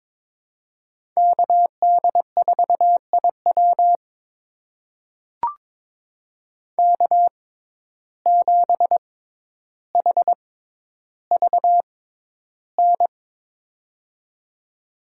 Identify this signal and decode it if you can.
Morse code, heard as a keyed tone, about 22 wpm, spaced as for 11 wpm, 700 Hz. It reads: KD4IW K7HVN
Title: USA Call Signs - 5 Characters - 6x Character Spacing - 22wpm